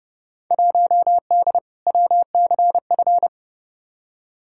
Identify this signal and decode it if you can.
Morse code, heard as a keyed tone, about 32 wpm, 700 Hz. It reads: 1D WCF
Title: Field Day Exchanges - 30wpm